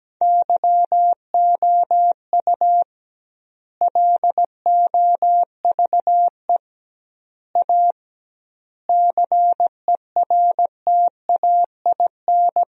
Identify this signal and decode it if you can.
Morse code, heard as a keyed tone, about 17 wpm, 700 Hz. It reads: YOU LOVE A CERTAIN
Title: 2x Word-Spacing - Sentences from Top 500 Words - 17wpm